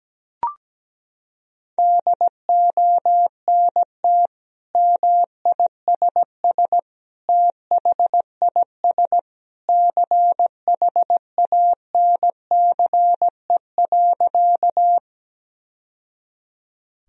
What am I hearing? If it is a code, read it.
Morse code, heard as a keyed tone, about 17 wpm, 700 Hz. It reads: DONT MISS THIS CHANCE.
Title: Sentences from Top 1000 Words - Review Part 3 - 17wpm